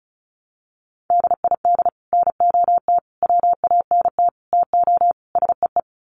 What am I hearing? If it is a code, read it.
Morse code, heard as a keyed tone, about 34 wpm, 700 Hz. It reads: DID NOT WANT TO SEE
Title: Sets of 5 Words Encore2 35wpm